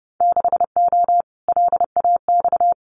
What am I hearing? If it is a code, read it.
Morse code, heard as a keyed tone, about 30 wpm, 700 Hz. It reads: 6O LAX
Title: Winter Field Day Exchanges Encore 30wpm